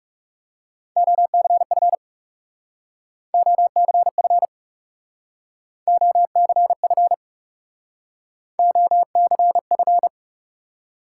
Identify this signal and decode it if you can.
Morse code, heard as a keyed tone, about 39 wpm, 700 Hz. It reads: OCF OCF OCF OCF